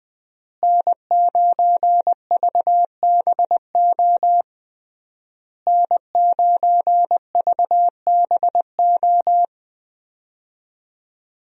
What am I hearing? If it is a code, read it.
Morse code, heard as a keyed tone, about 20 wpm, 700 Hz. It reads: N9VBO N9VBO
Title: USA Call Signs - 5 Characters - Repeated 3x - 20wpm